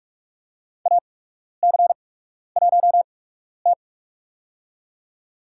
Morse code, heard as a keyed tone, about 46 wpm, spaced as for 12 wpm, 700 Hz. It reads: AC1T
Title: USA Call Signs - 4 Characters - 8x Character Spacing - 45wpm